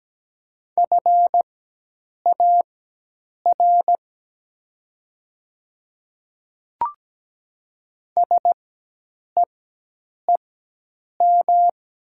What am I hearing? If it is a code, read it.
Morse code, heard as a keyed tone, about 17 wpm, spaced as for 8 wpm, 700 Hz. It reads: FAR SEEM
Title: 4x Character Spacing - Top 300 Words - 17wpm